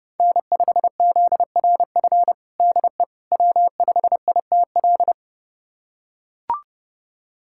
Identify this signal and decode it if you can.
Morse code, heard as a keyed tone, about 32 wpm, 700 Hz. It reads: N5ZRF DE W5ITL
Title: USA Callsign DE Callsign - 5 Characters - 30wpm